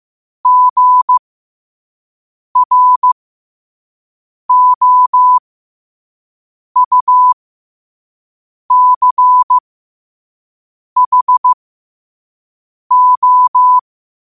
Morse code, heard as a keyed tone, about 15 wpm, spaced as for 5 wpm, 1000 Hz. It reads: GROUCHO
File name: qotdfiles / 5wpm